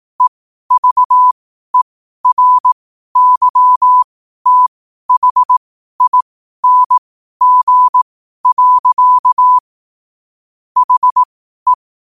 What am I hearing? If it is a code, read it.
Morse code, heard as a keyed tone, about 18 wpm, spaced as for 12 wpm, 1000 Hz. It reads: EVERYTHING. HE
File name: qotdfiles / 12wpm